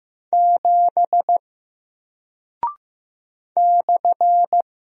Morse code, heard as a keyed tone, about 15 wpm, 700 Hz. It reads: 7 /
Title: Single Character (Letter:Number:Punctuation:Prosign) Alt 15wpm